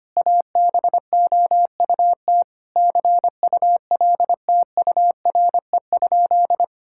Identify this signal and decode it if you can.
Morse code, heard as a keyed tone, about 25 wpm, 700 Hz. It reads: ABOUT CULTURE?
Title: Sentences from Top 1000 Words - Review Part 6 - 25wpm